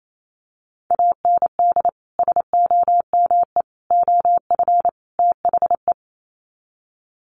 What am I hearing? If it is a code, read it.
Morse code, heard as a keyed tone, about 28 wpm, 700 Hz. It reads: AND SOME OF THE
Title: Sets of 4 Words 28wpm